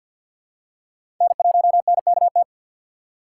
Morse code, heard as a keyed tone, about 50 wpm, 700 Hz. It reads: N1NKT
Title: USA Call Signs - 3x Character Spacing - 50wpm